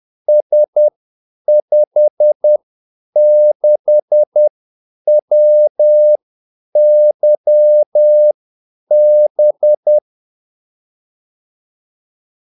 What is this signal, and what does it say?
Morse code, heard as a keyed tone, about 10 wpm, spaced as for 8 wpm, 600 Hz. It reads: S56WYB
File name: Callsigns-10-8wpm